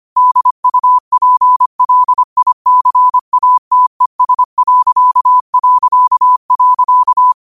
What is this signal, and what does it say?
Morse code, heard as a keyed tone, about 25 wpm, 1000 Hz. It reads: DUPLICATES...
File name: qotdfiles / 25wpm